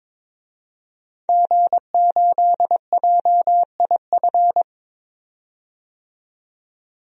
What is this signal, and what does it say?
Morse code, heard as a keyed tone, about 22 wpm, 700 Hz. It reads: G8JIF
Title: DXCC-100 Call Signs - 5 Characters - 22wpm